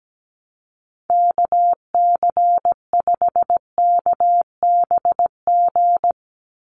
Morse code, heard as a keyed tone, about 17 wpm, 700 Hz. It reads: KC5KBG
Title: USA Call Signs 17wpm